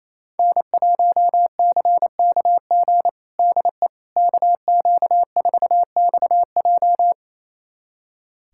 Morse code, heard as a keyed tone, about 28 wpm, 700 Hz. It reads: N1CKG DE KQ4XJ
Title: USA Callsign DE Callsign - 5 Characters - 28wpm